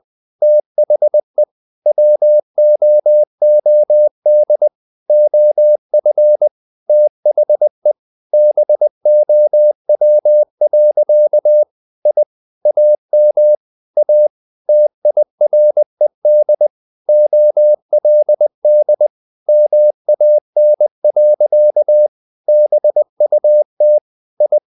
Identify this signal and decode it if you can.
Morse code, heard as a keyed tone, about 20 wpm, 600 Hz. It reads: THE WOOD OF THE BOW. I AM A TIRED OLD MAN. BUT I
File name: the old man and the sea part 2 20wpm